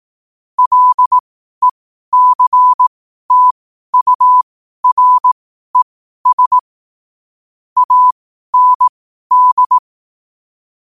Morse code, heard as a keyed tone, about 18 wpm, spaced as for 12 wpm, 1000 Hz. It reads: LECTURES AND
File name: qotdfiles / 12wpm